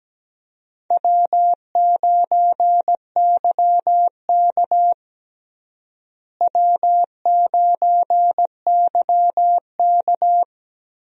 Morse code, heard as a keyed tone, about 17 wpm, 700 Hz. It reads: W9YK W9YK
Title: USA Call Signs - 4 Characters - Repeated 3x - 17wpm